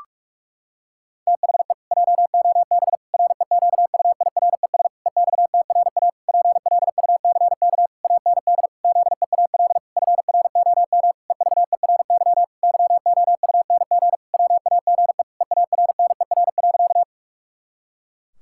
Morse code, heard as a keyed tone, about 46 wpm, 700 Hz. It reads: THE JOB REQUIRES EXTRA PLUCK AND ZEAL FROM EVERY YOUNG WAGE EARNER.
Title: Pangram Challenge v2 - 45wpm